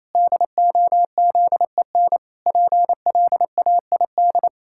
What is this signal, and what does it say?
Morse code, heard as a keyed tone, about 28 wpm, 700 Hz. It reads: DOZEN PLAID
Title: Pangram Challenge v2 - 28wpm